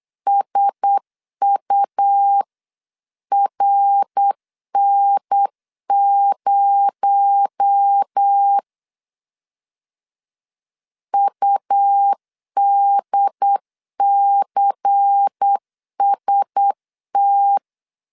Morse code, heard as a keyed tone, about 8 wpm, 800 Hz. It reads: SURN0 UDCST
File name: 8wpm